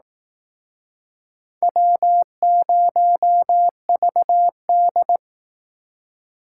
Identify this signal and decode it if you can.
Morse code, heard as a keyed tone, about 19 wpm, 700 Hz. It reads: W0VD
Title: Speed Racing - USA 1x2 Call Signs 15wpm